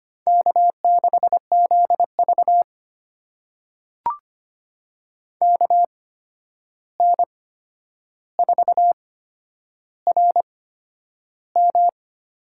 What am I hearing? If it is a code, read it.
Morse code, heard as a keyed tone, about 25 wpm, spaced as for 10 wpm, 700 Hz. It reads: K6ZV KN4RM
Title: USA Call Signs - 8x Character Spacing - 25wpm